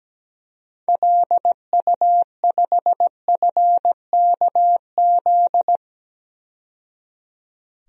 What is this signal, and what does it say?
Morse code, heard as a keyed tone, about 17 wpm, 700 Hz. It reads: LU5FKZ
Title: DXCC-100 Call Signs - 6 Characters - 17wpm